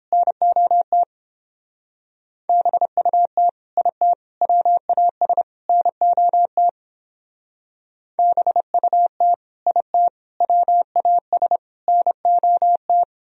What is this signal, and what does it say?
Morse code, heard as a keyed tone, about 29 wpm, 700 Hz. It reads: NOT BUT IT WAS NOT BUT IT WAS NOT